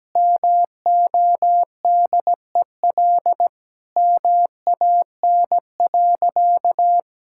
Morse code, heard as a keyed tone, about 17 wpm, 700 Hz. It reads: MODEL MAN.
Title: Sentences from Top 1000 Words - Review Part 5 - 17wpm